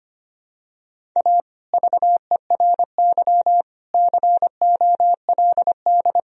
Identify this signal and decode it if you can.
Morse code, heard as a keyed tone, about 25 wpm, 700 Hz. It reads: A VERY COLD
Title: Sentences from Top 1000 Words - Review Part 1 - 25wpm